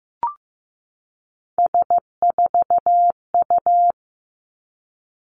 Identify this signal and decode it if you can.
Morse code, heard as a keyed tone, about 15 wpm, 700 Hz. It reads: S4U
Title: Three Challenging Letter-Number Combinations 15wpm